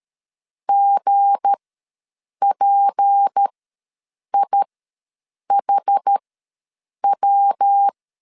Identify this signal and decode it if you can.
Morse code, heard as a keyed tone, about 13 wpm, spaced as for 7 wpm, 800 Hz. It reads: GPIHW
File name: LEZIONI / 7 GF9Z / 7 G ws 12wpm 3140